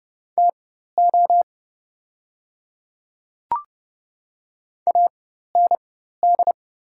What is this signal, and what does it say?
Morse code, heard as a keyed tone, about 30 wpm, spaced as for 14 wpm, 700 Hz. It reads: TO AND
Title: 4x Character Spacing - Top 100 Words - 30wpm